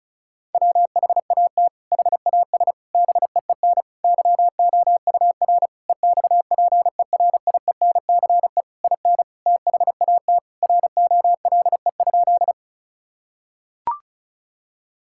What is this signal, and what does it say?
Morse code, heard as a keyed tone, about 36 wpm, 700 Hz. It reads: WHAT HAS BEEN YOUR EXPERIENCE IN THAT ROLE?
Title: Sentences from Top 1000 Words - Review Part 6 - 35wpm